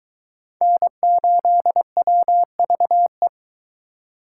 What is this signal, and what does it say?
Morse code, heard as a keyed tone, about 22 wpm, 700 Hz. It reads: N8WVE